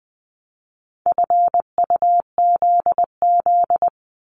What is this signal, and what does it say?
Morse code, heard as a keyed tone, about 20 wpm, 700 Hz. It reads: FUZZ
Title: Multi-letter Phonemes 20wpm